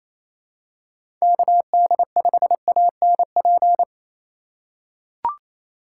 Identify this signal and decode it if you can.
Morse code, heard as a keyed tone, about 28 wpm, 700 Hz. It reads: KD5ANP
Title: USA Call Signs - 6 Characters - 2x Character Spacing - 28wpm